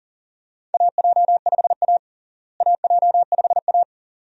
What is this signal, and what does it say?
Morse code, heard as a keyed tone, about 40 wpm, 700 Hz. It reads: AJ5A AJ5A
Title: USA Call Signs - 4 Characters - Repeated 2x - 40wpm